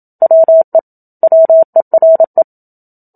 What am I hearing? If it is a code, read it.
Morse code, heard as a keyed tone, about 27 wpm, 650 Hz. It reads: WE WERE
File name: Book Two / 01-Chapter 1 - 27 WPM